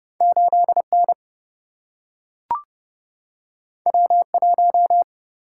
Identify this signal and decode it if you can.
Morse code, heard as a keyed tone, about 31 wpm, 700 Hz. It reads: 8 N W 1
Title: Single-Letter Single-Number 30wpm